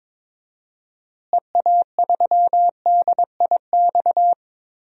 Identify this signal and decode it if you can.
Morse code, heard as a keyed tone, about 22 wpm, 700 Hz. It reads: EA3DIX
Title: DXCC-100 Call Signs - 6 Characters - Repeated 3x - 22wpm